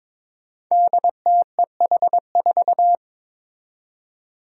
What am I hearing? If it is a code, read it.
Morse code, heard as a keyed tone, about 22 wpm, 700 Hz. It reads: DTEH4
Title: Five Challenging Letter-Number Combinations 22wpm